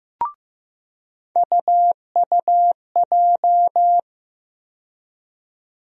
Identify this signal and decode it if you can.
Morse code, heard as a keyed tone, about 15 wpm, 700 Hz. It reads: UUJ
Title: USA Call Sign Postfixes - 3 Letters 15wpm